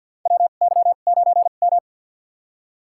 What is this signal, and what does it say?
Morse code, heard as a keyed tone, about 52 wpm, spaced as for 36 wpm, 700 Hz. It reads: WY9M
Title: USA Call Signs - 2x Character Spacing - 50wpm